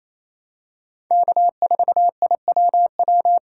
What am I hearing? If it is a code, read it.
Morse code, heard as a keyed tone, about 28 wpm, 700 Hz. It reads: K4IWW
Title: USA Call Signs - 5 Characters - 4x Character Spacing - 28wpm